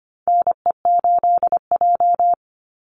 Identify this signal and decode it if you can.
Morse code, heard as a keyed tone, about 25 wpm, 700 Hz. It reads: NE8J
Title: Call Signs - 25wpm